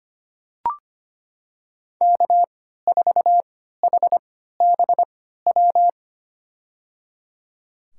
Morse code, heard as a keyed tone, about 25 wpm, spaced as for 14 wpm, 700 Hz. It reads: K4HBW
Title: USA Call Signs - 5 Characters - 3x Character Spacing - 25wpm